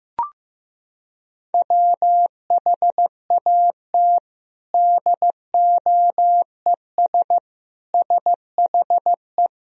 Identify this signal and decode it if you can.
Morse code, heard as a keyed tone, about 15 wpm, 700 Hz. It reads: WHAT DOES SHE
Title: Sentences from Top 200 Words and Abbreviations 15wpm